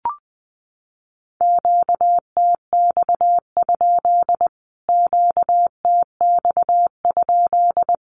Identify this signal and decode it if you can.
Morse code, heard as a keyed tone, about 20 wpm, 700 Hz. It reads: QTX? QTX?
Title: Q-Codes Repeated 3x 20wpm